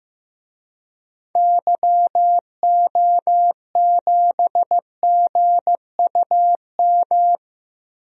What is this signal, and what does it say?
Morse code, heard as a keyed tone, about 15 wpm, 700 Hz. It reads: YO7GUM
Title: DXCC-100 Call Signs - 6 Characters - Repeated 2x - 15wpm